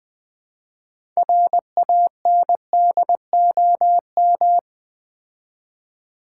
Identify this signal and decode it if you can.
Morse code, heard as a keyed tone, about 20 wpm, 700 Hz. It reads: RANDOM
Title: Build A Word - 2 Letters At A Time - 6-Letter Words - 20wpm